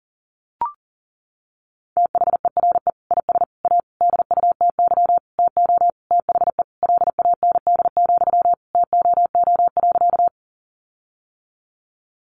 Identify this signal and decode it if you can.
Morse code, heard as a keyed tone, about 40 wpm, 700 Hz. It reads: THERE IS A DUTY TO THE LAND, TOO.
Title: Sentences from Top 1500 Words 40wpm